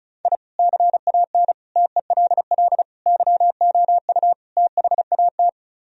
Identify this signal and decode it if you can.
Morse code, heard as a keyed tone, about 36 wpm, 700 Hz. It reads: I CAN TELL YOU THAT
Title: Speed-Racing - Sets of 5 Words 30wpm